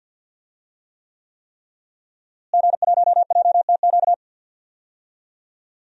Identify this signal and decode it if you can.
Morse code, heard as a keyed tone, about 48 wpm, 700 Hz. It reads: G1JTQ
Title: DXCC-100 Call Signs - 5 Characters - Repeated 3x - 50wpm